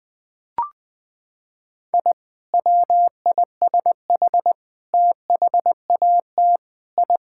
Morse code, heard as a keyed tone, about 21 wpm, 700 Hz. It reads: I WISH THAT I
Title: Sentences with 2-Word Phrases - 20wpm